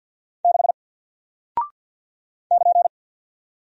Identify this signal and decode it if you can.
Morse code, heard as a keyed tone, about 48 wpm, 700 Hz. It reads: 6 <KN>
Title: Single Character (Letter:Number:Punctuation:Prosign) Alt 50wpm